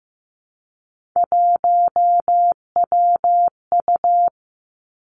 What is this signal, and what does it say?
Morse code, heard as a keyed tone, about 15 wpm, 700 Hz. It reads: 1WU
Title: Three Challenging Letter-Number Combinations 15wpm